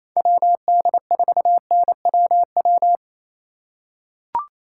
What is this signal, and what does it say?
Morse code, heard as a keyed tone, about 28 wpm, 700 Hz. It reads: WD4NWW
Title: USA Call Signs - 6 Characters - 3x Character Spacing - 28wpm